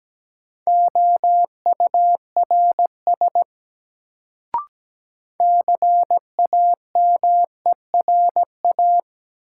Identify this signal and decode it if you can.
Morse code, heard as a keyed tone, about 17 wpm, 700 Hz. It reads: OURS CAMERA
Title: Sentences from Top 1000 Words - Review Part 5 - New Word Drill - 17wpm